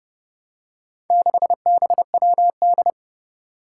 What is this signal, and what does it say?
Morse code, heard as a keyed tone, about 28 wpm, 690 Hz. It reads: 6BWD
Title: Four Challenging Letter-Number Combinations 30wpm